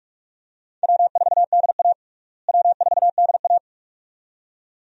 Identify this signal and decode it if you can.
Morse code, heard as a keyed tone, about 41 wpm, 700 Hz. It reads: W4DA W4DA
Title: USA Call Signs - 4 Characters - Repeated 2x - 45wpm